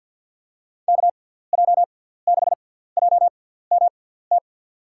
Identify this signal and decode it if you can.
Morse code, heard as a keyed tone, about 50 wpm, spaced as for 17 wpm, 700 Hz. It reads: KJ6JMT
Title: USA Call Signs - 6 Characters - 6x Character Spacing - 50wpm